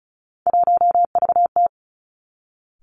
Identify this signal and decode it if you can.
Morse code, heard as a keyed tone, about 34 wpm, 700 Hz. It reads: 1VT
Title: Three Challenging Letter-Number Combinations 35wpm